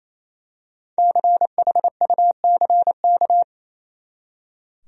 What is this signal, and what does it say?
Morse code, heard as a keyed tone, about 28 wpm, 700 Hz. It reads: CHUCK
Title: USA Common Names - 28wpm